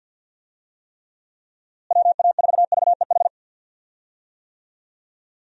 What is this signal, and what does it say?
Morse code, heard as a keyed tone, about 46 wpm, 700 Hz. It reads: WA4VEH
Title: USA Call Signs - 6 Characters - Repeated 2x - 50wpm